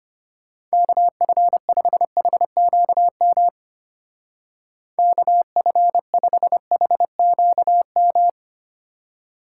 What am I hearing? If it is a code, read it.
Morse code, heard as a keyed tone, about 31 wpm, 700 Hz. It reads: KF5HQM KF5HQM
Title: Speed Racing - USA 2x3 Call Signs 25wpm